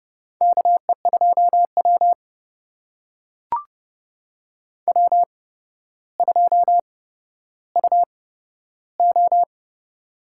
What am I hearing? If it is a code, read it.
Morse code, heard as a keyed tone, about 30 wpm, spaced as for 13 wpm, 700 Hz. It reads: KE2W W2UO
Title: USA Call Signs - 4 Characters - 8x Character Spacing - 30wpm